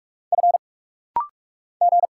Morse code, heard as a keyed tone, about 44 wpm, 700 Hz. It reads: F G
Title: Single Letter-Number Alt - Rapid-Fire 45wpm